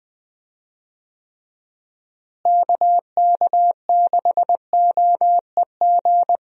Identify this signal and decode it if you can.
Morse code, heard as a keyed tone, about 21 wpm, 700 Hz. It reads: KK6OEG
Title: USA Call Signs - 6 Characters - Repeated 3x - 20wpm